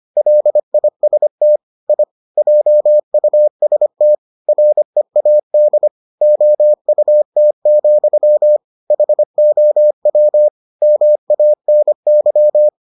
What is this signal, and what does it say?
Morse code, heard as a keyed tone, about 25 wpm, 600 Hz. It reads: LIST I JUST READ OUT, HOW MANY